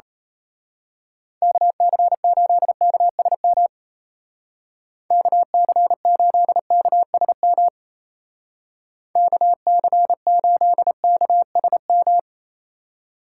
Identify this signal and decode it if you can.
Morse code, heard as a keyed tone, about 33 wpm, 700 Hz. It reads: KC8KSM KC8KSM KC8KSM
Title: Speed Racing - USA Call Signs 28wpm